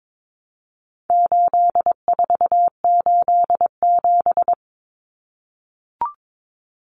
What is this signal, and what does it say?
Morse code, heard as a keyed tone, about 22 wpm, 700 Hz. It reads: 8487
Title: Common Cut Numbers - 4-Digit 22wpm